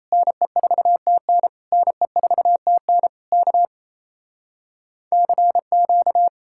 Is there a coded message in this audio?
Yes: Morse code from 0.1 to 6.3 s, heard as a keyed tone, about 37 wpm, 700 Hz. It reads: NE4TN NE4TN K CQ